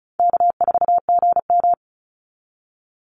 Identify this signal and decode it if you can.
Morse code, heard as a keyed tone, about 34 wpm, 700 Hz. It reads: K4GM
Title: Easy Call Signs Repeated 3x - 35wpm